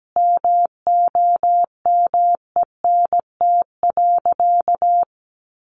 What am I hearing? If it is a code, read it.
Morse code, heard as a keyed tone, about 17 wpm, 700 Hz. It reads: MOMENT.
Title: Sentences from Top 1300 Words 17wpm